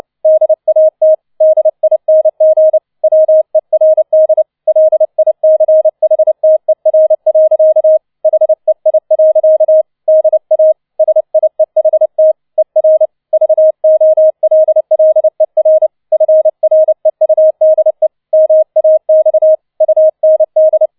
Morse code, heard as a keyed tone, about 29 wpm, 630 Hz. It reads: DAT DING WERD LICHTER. HEI. DA SIEHT ER VOLLER FREUDE MAX UND